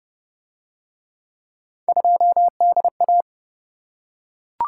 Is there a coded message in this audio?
Yes: Morse code from 1.9 to 3.2 s, heard as a keyed tone, about 27 wpm, 700 Hz. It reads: 2DA